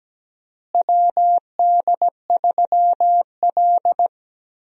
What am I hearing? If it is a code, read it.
Morse code, heard as a keyed tone, about 17 wpm, 700 Hz. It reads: WD3L
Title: USA Call Signs - 4 Characters - Repeated 2x - 17wpm